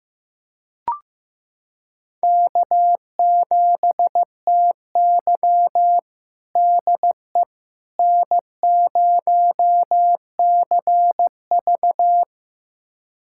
Morse code, heard as a keyed tone, about 16 wpm, 700 Hz. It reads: K7TY DE N0CV
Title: USA Callsign DE Callsign - 4 Characters - 15wpm